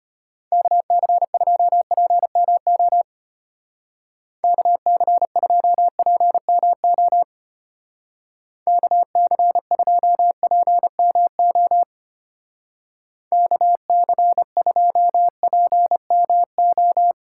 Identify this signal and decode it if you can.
Morse code, heard as a keyed tone, about 32 wpm, 700 Hz. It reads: KC2PMO KC2PMO KC2PMO KC2PMO